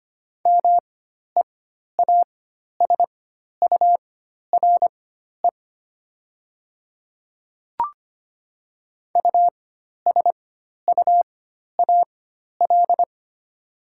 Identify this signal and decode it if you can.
Morse code, heard as a keyed tone, about 25 wpm, spaced as for 12 wpm, 700 Hz. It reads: MEASURE USUAL
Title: 4x Character Spacing - Top 500 Words - 25wpm